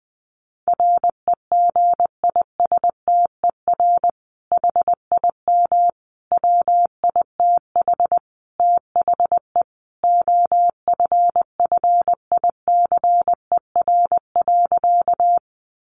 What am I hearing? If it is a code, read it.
Morse code, heard as a keyed tone, about 19 wpm, 700 Hz. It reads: REGISTER HIM WITH THE OFFICER.
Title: Sentences from Top 1200 Words 20wpm